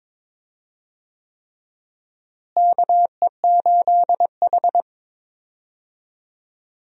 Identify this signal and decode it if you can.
Morse code, heard as a keyed tone, about 22 wpm, 700 Hz. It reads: KE8H
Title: USA Call Signs - 4 Characters - Repeated 3x - 22wpm